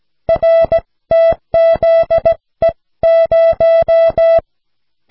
Morse code, heard as a keyed tone, about 17 wpm, spaced as for 15 wpm, 660 Hz. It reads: RTZE0
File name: cw / snelheid 4